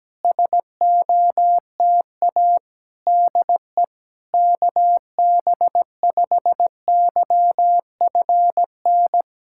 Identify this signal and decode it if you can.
Morse code, heard as a keyed tone, about 17 wpm, 700 Hz. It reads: SOTA DE KB5YFN
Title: SOTA QSO Practice 17wpm